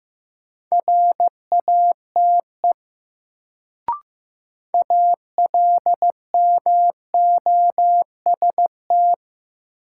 Morse code, heard as a keyed tone, about 15 wpm, 700 Hz. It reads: RATE ALMOST
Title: Sentences from Top 700 Words - New Word Drill - 15wpm